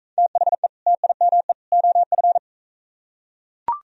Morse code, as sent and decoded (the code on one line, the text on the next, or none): - .... / - .. -- . / --- ..-.
THE TIME OF